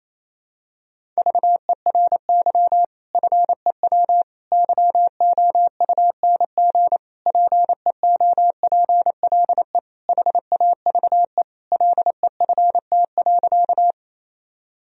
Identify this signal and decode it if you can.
Morse code, heard as a keyed tone, about 28 wpm, 700 Hz. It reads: VERY FEW YOUNG PEOPLE HAVE LEFT.
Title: Sentences from Top 1000 Words - Review Part 1 - 28wpm